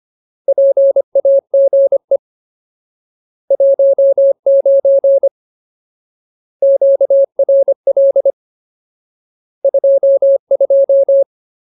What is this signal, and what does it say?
Morse code, heard as a keyed tone, about 25 wpm, 550 Hz. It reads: PAGE 19 QRL 22
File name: Ultimate Word List / 25WPM Ultimate